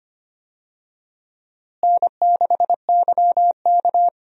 Morse code, heard as a keyed tone, about 25 wpm, 700 Hz. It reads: N6YK
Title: USA Call Signs - 4 Characters - Repeated 3x - 25wpm